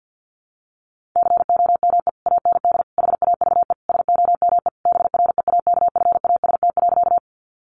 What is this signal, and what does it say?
Morse code, heard as a keyed tone, about 50 wpm, 700 Hz. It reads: COME AND HAVE SOME BREAKFAST.
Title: Sentences from Top 1500 Words 50wpm